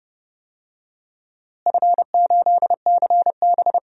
Morse code, heard as a keyed tone, about 30 wpm, 700 Hz. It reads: F8CB
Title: DXCC-100 Call Signs - 4 Characters - Repeated 2x - 30wpm